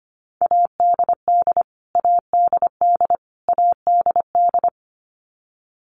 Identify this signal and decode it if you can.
Morse code, heard as a keyed tone, about 25 wpm, 700 Hz. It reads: ADD ADD ADD
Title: Top 500 Words Repeated 25wpm